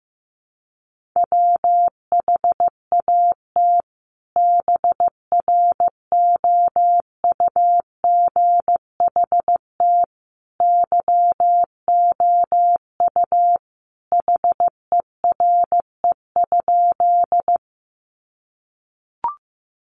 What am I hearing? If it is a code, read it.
Morse code, heard as a keyed tone, about 15 wpm, 700 Hz. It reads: WHAT BROUGHT YOU HERE?
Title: Sentences from Top 500 Words 15wpm